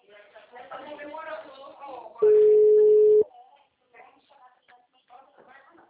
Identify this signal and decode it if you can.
Ringback tone (Europe, 425 Hz), with speech.
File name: out- -2030